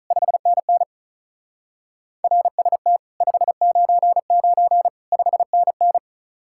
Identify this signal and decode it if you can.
Morse code, heard as a keyed tone, about 44 wpm, 700 Hz. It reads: 5NN RST 599 5NN